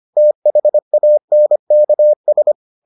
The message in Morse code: THANKS

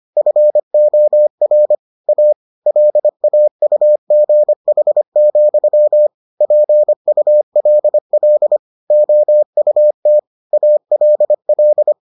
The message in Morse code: FOR A LAUGH, PULL OUT ALL

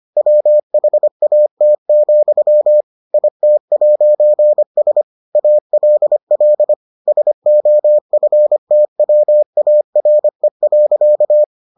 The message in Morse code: WHAT, IT'S ALL SOFTWARE.

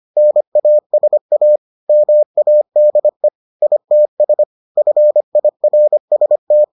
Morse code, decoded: NASA MADE ITS FIRST

